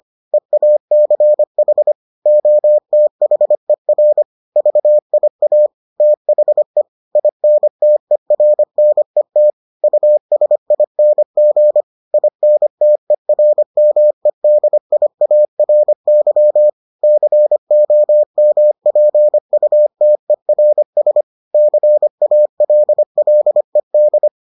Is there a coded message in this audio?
Yes: Morse code from 0.3 to 24.3 s, heard as a keyed tone, about 25 wpm, 600 Hz. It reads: EACH OTHER VIA THE INTERNET USING INTERMEDIARY COMPUTERS CALLED